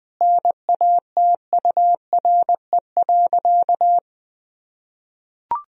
Morse code, heard as a keyed tone, about 20 wpm, 700 Hz. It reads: NATURE.
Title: Sentences from Top 1000 Words - Review Part 4 - 20wpm